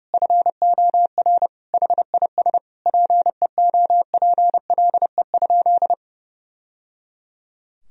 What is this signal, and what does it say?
Morse code, heard as a keyed tone, about 32 wpm, 700 Hz. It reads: FOR HIS PEOPLE?
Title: Sentences from Top 1000 Words - Review Part 3 - 30wpm